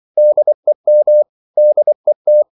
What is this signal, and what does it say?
Morse code, heard as a keyed tone, about 24 wpm, 600 Hz. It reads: DEM DET
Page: Ferie 23 wpm